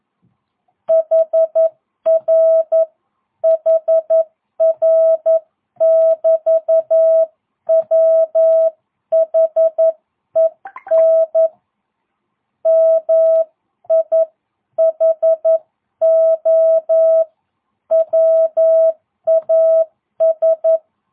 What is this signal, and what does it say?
Morse code, heard as a keyed tone, about 11 wpm, 650 Hz. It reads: HR HR <BT> WHEN MIHO WAS